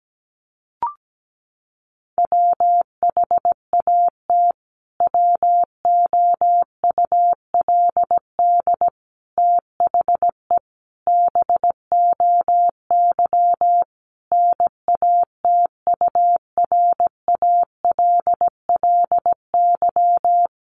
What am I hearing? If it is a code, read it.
Morse code, heard as a keyed tone, about 17 wpm, 700 Hz. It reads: WHAT WOULD THE BOY NATURALLY
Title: Sentences from Top 1300 Words 17wpm